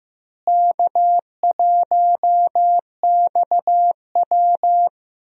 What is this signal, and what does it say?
Morse code, heard as a keyed tone, about 15 wpm, 700 Hz. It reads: K1XW
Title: USA Call Signs - 4 Characters - Repeated 2x - 15wpm